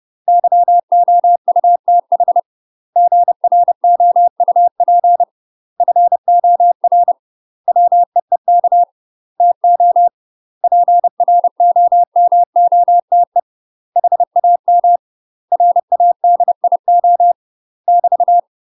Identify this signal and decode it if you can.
Morse code, heard as a keyed tone, about 30 wpm, 700 Hz. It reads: YOUTH GROUP FOR WEEK TO PROMOTE HAM RADIO <BT>
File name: UK MASTER QSOs TRAINING / 30WPM / UK MASTER QSOs 0039